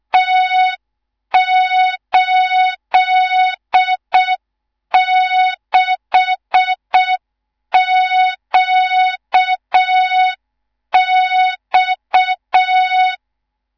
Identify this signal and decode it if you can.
Morse code, heard as a keyed tone, about 6 wpm, 3700 Hz. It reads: T86QX
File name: Morze-HA2MN / hang / 18 IRASJELEK2